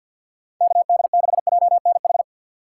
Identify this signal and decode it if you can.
Morse code, heard as a keyed tone, about 53 wpm, 690 Hz. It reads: KD6JNH
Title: USA Call Signs - 2x Character Spacing - 50wpm